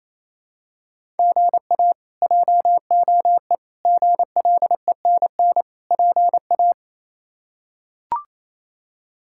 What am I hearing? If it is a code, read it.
Morse code, heard as a keyed tone, about 28 wpm, 700 Hz. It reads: GA JOE GLENN PA
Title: K1USN SST QSO Practice - 28wpm